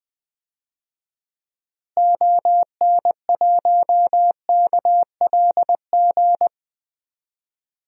Morse code, heard as a keyed tone, about 20 wpm, 700 Hz. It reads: ON1KLG
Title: DXCC-100 Call Signs - 6 Characters - Repeated 3x - 20wpm